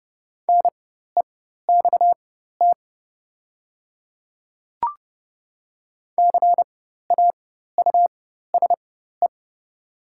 Morse code, heard as a keyed tone, about 30 wpm, spaced as for 14 wpm, 700 Hz. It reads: NEXT CAUSE
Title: 4x Character Spacing - Top 500 Words - 30wpm